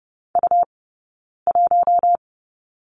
Morse code, heard as a keyed tone, about 30 wpm, 700 Hz. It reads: U 1
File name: Lesson 17d - Letter u - Two Characters 30wpm